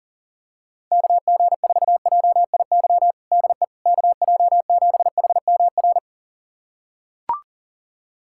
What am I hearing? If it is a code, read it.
Morse code, heard as a keyed tone, about 37 wpm, 700 Hz. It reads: KG4JIY DE KJ7HMR
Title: USA Callsign DE Callsign 40wpm